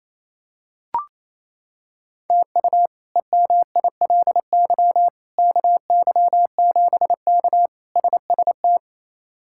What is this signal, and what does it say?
Morse code, heard as a keyed tone, about 28 wpm, 700 Hz. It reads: TU EMILY KY7K SST
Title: K1USN SST QSO Practice - 28wpm